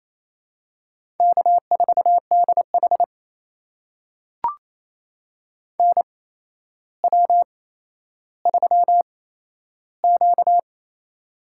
Morse code, heard as a keyed tone, about 28 wpm, spaced as for 12 wpm, 700 Hz. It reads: K4DH NW3Q
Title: USA Call Signs - 4 Characters - 8x Character Spacing - 28wpm